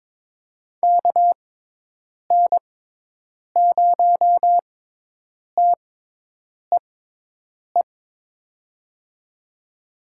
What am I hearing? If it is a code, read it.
Morse code, heard as a keyed tone, about 22 wpm, spaced as for 8 wpm, 700 Hz. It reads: KN0TEE
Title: USA Call Signs - 6 Characters - 6x Character Spacing - 22wpm